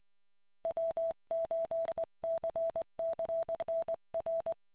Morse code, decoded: W8CCN R